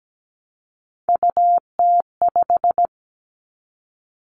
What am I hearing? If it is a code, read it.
Morse code, heard as a keyed tone, about 17 wpm, 700 Hz. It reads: UT5
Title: Callsign Prefixes for Top 25 DXCC - 17wpm